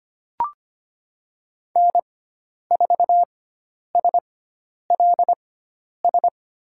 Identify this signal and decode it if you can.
Morse code, heard as a keyed tone, about 25 wpm, spaced as for 10 wpm, 700 Hz. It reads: N4SLS
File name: Lesson 16g - Letter c - Call Signs 25wpm